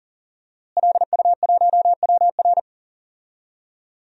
Morse code, heard as a keyed tone, about 39 wpm, 700 Hz. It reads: LU1WR
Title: DXCC-100 Call Signs - 5 Characters - Repeated 3x - 40wpm